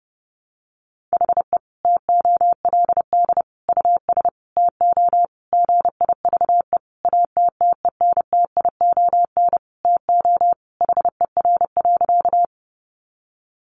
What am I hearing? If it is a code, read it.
Morse code, heard as a keyed tone, about 30 wpm, 700 Hz. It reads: HE TOLD US TO GIVE ATTENTION TO HER.
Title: Sentences from Top 1000 Words 30wpm